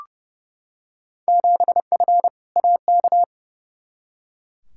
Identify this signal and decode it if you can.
Morse code, heard as a keyed tone, about 30 wpm, 700 Hz. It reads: 7F AK
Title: Field Day Exchanges Encore - 30wpm